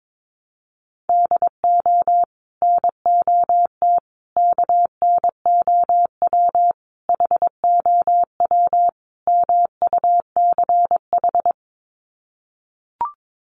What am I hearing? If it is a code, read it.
Morse code, heard as a keyed tone, about 22 wpm, 700 Hz. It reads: DO NOT KNOW HOW MUCH
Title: Sets of 5 Words Encore 22wpm